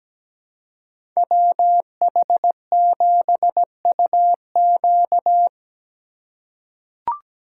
Morse code, heard as a keyed tone, about 17 wpm, 700 Hz. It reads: WH7UQ
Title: USA Call Signs - 5 Characters - 2x Character Spacing - 17wpm